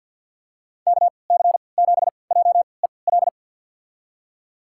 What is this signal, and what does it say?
Morse code, heard as a keyed tone, about 48 wpm, spaced as for 28 wpm, 700 Hz. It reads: KC7JEL
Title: USA Call Signs - 6 Characters - 3x Character Spacing - 50wpm